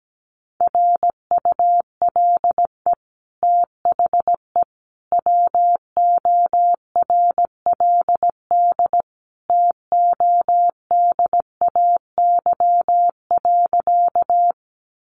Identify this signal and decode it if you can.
Morse code, heard as a keyed tone, about 17 wpm, 700 Hz. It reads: RULE THE WORLD TODAY.
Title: Sentences from Top 600 Words 17wpm